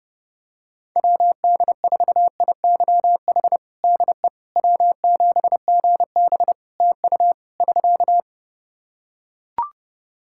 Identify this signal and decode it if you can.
Morse code, heard as a keyed tone, about 30 wpm, 700 Hz. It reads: WD4IYH DE W7GB TU <SK>
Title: Practice QSO - Exchange 4 - Ending - 30wpm